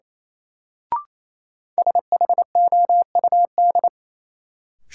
Morse code, heard as a keyed tone, about 28 wpm, 700 Hz. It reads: SHOUD
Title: Sentences from Top 1000 Words - Review Part 4 - New Word Drill - 28wpm